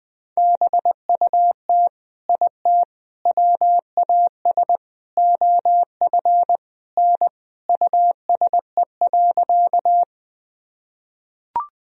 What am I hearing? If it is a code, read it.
Morse code, heard as a keyed tone, about 20 wpm, 700 Hz. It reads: BUT IT WAS OF N USE.
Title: Sentences from Top 100 Words and Abbreviations 20wpm